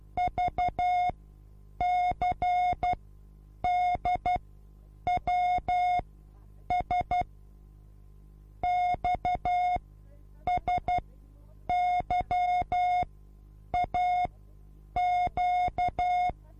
Morse code, heard as a keyed tone, about 12 wpm, spaced as for 8 wpm, 720 Hz. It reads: VCDWS XSYAQ
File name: repos / Morsekurs / DJ4UF Morsen24